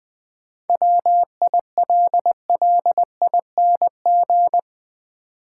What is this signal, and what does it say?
Morse code, heard as a keyed tone, about 20 wpm, 700 Hz. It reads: WILLING
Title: Sentences from Top 1000 Words - New Word Drill - 20wpm